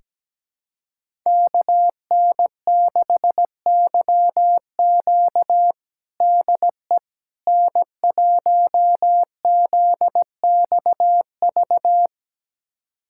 Morse code, heard as a keyed tone, about 17 wpm, 700 Hz. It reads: KN6YQ DE N1ZXV
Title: USA Callsign DE Callsign - 5 Characters - 17wpm